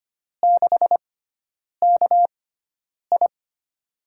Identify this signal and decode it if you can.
Morse code, heard as a keyed tone, about 25 wpm, spaced as for 9 wpm, 700 Hz. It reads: 6KI
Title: USA Call Signs - 4 Characters - 6x Character Spacing - 25wpm